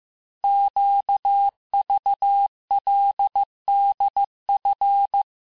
Morse code, heard as a keyed tone, about 15 wpm, 800 Hz. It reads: QVLDF
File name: cours-du-lundi files / cours-1585-texte-chiffre-vitesse-14mpm